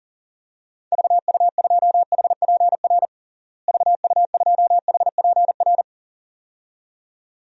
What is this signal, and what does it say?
Morse code, heard as a keyed tone, about 37 wpm, 700 Hz. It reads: VU2HPR VU2HPR
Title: DXCC-100 Call Signs - 6 Characters - Repeated 2x - 40wpm